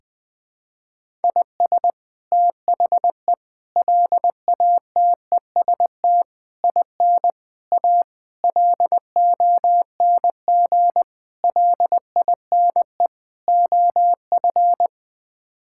Morse code, heard as a keyed tone, about 20 wpm, 700 Hz. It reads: IS THE LATEST IN A LONG LINE OF
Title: Sets of 8 Words - 20wpm